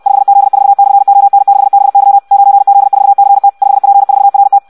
Morse code, heard as a keyed tone, about 64 wpm, 800 Hz. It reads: 5Z4/ON6BY 9Z4CT 5P4RT
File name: Audio practica CW / qrq-tx-66 2